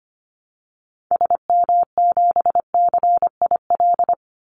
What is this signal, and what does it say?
Morse code, heard as a keyed tone, about 25 wpm, 700 Hz. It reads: SM7CIL
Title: Call Signs Repeated 2x - 25wpm